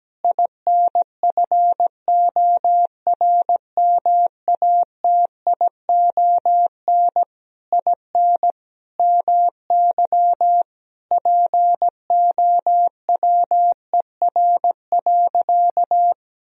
Morse code, heard as a keyed tone, about 17 wpm, 700 Hz. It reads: INFORMATION IN MY POWER.
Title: Sentences from Top 1000 Words - Review Part 3 - 17wpm